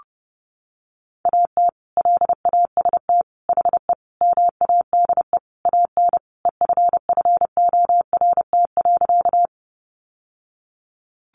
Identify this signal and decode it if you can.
Morse code, heard as a keyed tone, about 30 wpm, 700 Hz. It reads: AT LAST HE MADE AN EFFORT.
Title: Sentences from Top 1200 Words 30wpm